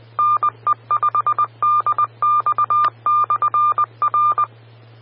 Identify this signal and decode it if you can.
Morse code, heard as a keyed tone, about 20 wpm, 1200 Hz. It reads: NE5DX/R
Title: Austin, TX Amateur Repeaters